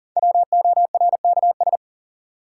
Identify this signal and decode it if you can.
Morse code, heard as a keyed tone, about 40 wpm, 700 Hz. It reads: WORKS
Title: Sentences from Top 1100 Words - New Word Drill Alt - 40wpm